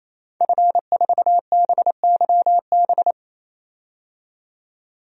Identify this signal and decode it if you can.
Morse code, heard as a keyed tone, about 28 wpm, 700 Hz. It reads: F4BYB
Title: DXCC-100 Call Signs - 5 Characters - 28wpm